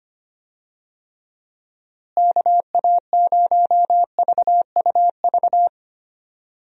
Morse code, heard as a keyed tone, about 25 wpm, 700 Hz. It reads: KA0VUV
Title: USA Call Signs - 6 Characters - Repeated 2x - 25wpm